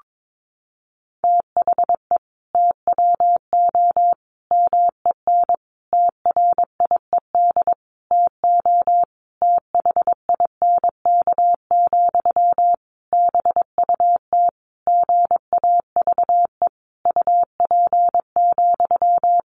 Morse code, heard as a keyed tone, about 22 wpm, 700 Hz. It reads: THE TWO MEN TRIED TO THINK, BUT GAVE UP,
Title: A Sound Of Thunder 22wpm